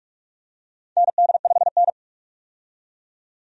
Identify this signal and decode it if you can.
Morse code, heard as a keyed tone, about 45 wpm, 700 Hz. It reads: ND5N
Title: USA Call Signs - 4 Characters - 45wpm